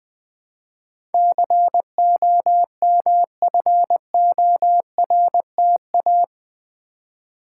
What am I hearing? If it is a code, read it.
Morse code, heard as a keyed tone, about 21 wpm, 700 Hz. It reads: COMFORTA
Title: Build A Word - 4 Letters At A Time - 11-Letter Words - 20wpm